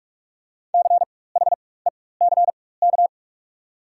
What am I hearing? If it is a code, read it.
Morse code, heard as a keyed tone, about 45 wpm, spaced as for 21 wpm, 690 Hz. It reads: CHECK
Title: 4x Character Spacing - Top 500 Words - 45wpm